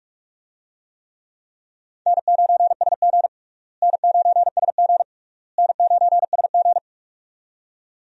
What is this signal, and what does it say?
Morse code, heard as a keyed tone, about 41 wpm, 700 Hz. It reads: N9SG N9SG N9SG
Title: USA Call Signs - 4 Characters - Repeated 3x - 45wpm